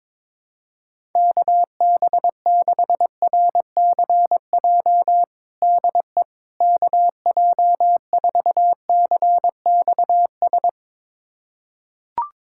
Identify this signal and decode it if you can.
Morse code, heard as a keyed tone, about 22 wpm, 700 Hz. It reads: KB6RCJ DE KJ4CXS
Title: USA Callsign DE Callsign - 6 Characters - 22wpm